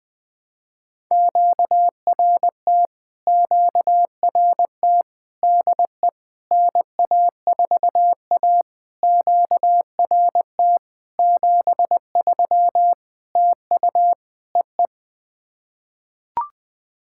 Morse code, heard as a keyed tone, about 20 wpm, 700 Hz. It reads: QRT QRT DE NA4A QRT 73 TU EE
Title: POTA QSO Practice - 20wpm